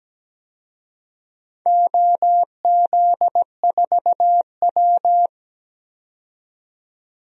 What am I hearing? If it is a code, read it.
Morse code, heard as a keyed tone, about 17 wpm, 700 Hz. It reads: OZ4W
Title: DXCC-100 Call Signs - 4 Characters - 17wpm